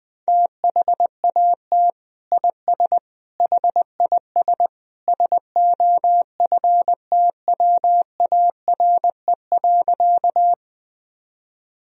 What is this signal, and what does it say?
Morse code, heard as a keyed tone, about 19 wpm, 700 Hz. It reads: THAT IS HIS SOFTWARE.
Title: Sentences from Top 1000 Words - Review Part 3 - 20wpm